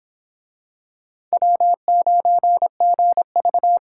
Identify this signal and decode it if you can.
Morse code, heard as a keyed tone, about 24 wpm, 700 Hz. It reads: W9GV